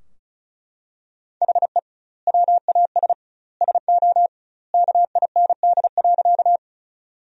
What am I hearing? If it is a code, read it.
Morse code, heard as a keyed tone, about 36 wpm, 700 Hz. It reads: HE WAS SO KIND.
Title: 2x Word-Spacing - Sentences from Top 200 Words - 35wpm